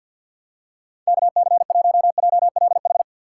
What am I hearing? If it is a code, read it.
Morse code, heard as a keyed tone, about 48 wpm, 700 Hz. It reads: KC1JLH
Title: USA Call Signs - 6 Characters - 6x Character Spacing - 50wpm